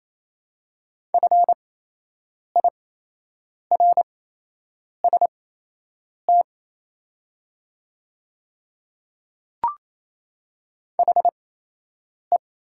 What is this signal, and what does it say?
Morse code, heard as a keyed tone, about 28 wpm, spaced as for 8 wpm, 700 Hz. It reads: FIRST HE
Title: 8x Character Spacing - Top 100 Words - 28wpm